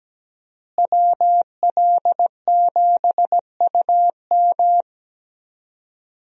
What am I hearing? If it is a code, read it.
Morse code, heard as a keyed tone, about 17 wpm, 700 Hz. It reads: WL7UM
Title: USA Call Signs - 5 Characters - Repeated 3x - 17wpm